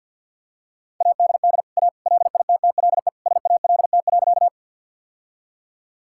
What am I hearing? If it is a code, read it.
Morse code, heard as a keyed tone, about 47 wpm, 700 Hz. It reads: ADD A LITTLE SALT.
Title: Sentences from Top 1000 Words - Review Part 5 - 50wpm